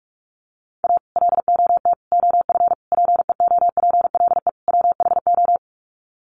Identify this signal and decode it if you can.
Morse code, heard as a keyed tone, about 44 wpm, 700 Hz. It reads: A LOT OF PEOPLE WHO
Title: Sets of 5 Words 45wpm